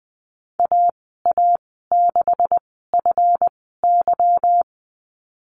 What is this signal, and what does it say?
Morse code, heard as a keyed tone, about 20 wpm, spaced as for 15 wpm, 700 Hz. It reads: AA6FY
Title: 173 Lesson 28g - Number 9 - Call Signs 20wpm